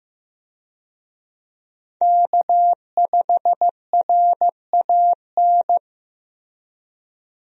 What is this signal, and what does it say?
Morse code, heard as a keyed tone, about 15 wpm, 700 Hz. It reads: K5RAN
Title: USA Call Signs - 5 Characters - Repeated 3x - 15wpm